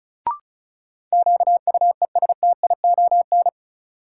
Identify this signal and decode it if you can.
Morse code, heard as a keyed tone, about 36 wpm, 700 Hz. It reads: QUESTION
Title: Sets of 5 Words Encore - New Word Drill - 35wpm